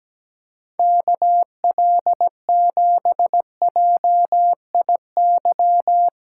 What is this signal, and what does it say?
Morse code, heard as a keyed tone, about 17 wpm, 700 Hz. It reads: KL7JIY
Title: USA Call Signs - 6 Characters - 6x Character Spacing - 17wpm